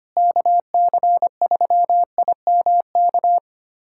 Morse code, heard as a keyed tone, about 25 wpm, 700 Hz. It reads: KC3IMK
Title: USA Callsign DE Callsign - 6 Characters - 25wpm